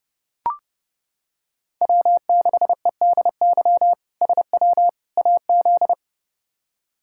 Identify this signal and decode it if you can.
Morse code, heard as a keyed tone, about 30 wpm, 700 Hz. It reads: W6EDY SW AZ
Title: K1USN SST QSO Practice - 30wpm